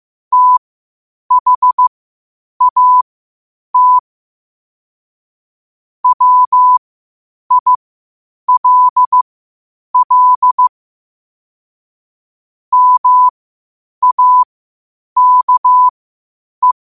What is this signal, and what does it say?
Morse code, heard as a keyed tone, about 15 wpm, spaced as for 8 wpm, 1000 Hz. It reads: THAT WILL MAKE
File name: qotdfiles / 8wpm